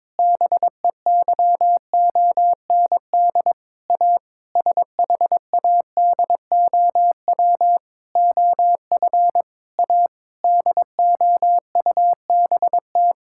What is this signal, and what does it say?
Morse code, heard as a keyed tone, about 22 wpm, 700 Hz. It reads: BEYOND A SHADOW OF A DOUBT
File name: Sentences with 7-Word Phrases - 22wpm